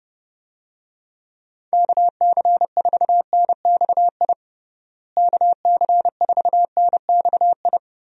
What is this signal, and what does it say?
Morse code, heard as a keyed tone, about 32 wpm, 700 Hz. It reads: KC4NXI KC4NXI
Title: USA Call Signs - 6 Characters - Repeated 2x - 30wpm